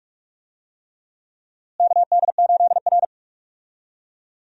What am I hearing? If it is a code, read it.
Morse code, heard as a keyed tone, about 46 wpm, 700 Hz. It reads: KD8R
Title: USA Call Signs - 4 Characters - Repeated 3x - 45wpm